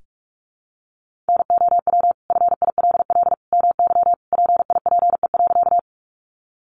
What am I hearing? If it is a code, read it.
Morse code, heard as a keyed tone, about 44 wpm, 700 Hz. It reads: NOW FILL MY PIPE.
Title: Sentences from Top 1500 Words 45wpm